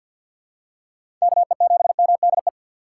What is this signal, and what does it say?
Morse code, heard as a keyed tone, about 46 wpm, 700 Hz. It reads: KE7MDE
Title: USA Call Signs - 6 Characters - 3x Character Spacing - 50wpm